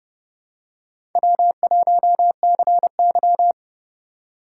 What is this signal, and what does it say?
Morse code, heard as a keyed tone, about 31 wpm, 700 Hz. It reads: W1CY
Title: USA Call Signs - 4 Characters - 8x Character Spacing - 30wpm